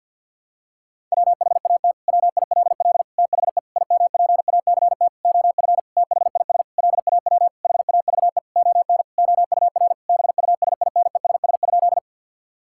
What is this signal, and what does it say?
Morse code, heard as a keyed tone, about 50 wpm, 700 Hz. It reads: WHAT WILL THE IMPACT OF THIS LAW HAVE ON OUR BUSINESS?
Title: Sentences from Top 1000 Words - Review Part 6 - 50wpm